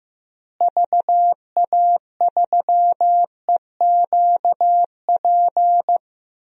Morse code, heard as a keyed tone, about 15 wpm, 700 Hz. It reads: VA3EQP
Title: DXCC-100 Call Signs - 6 Characters - Repeated 3x - 15wpm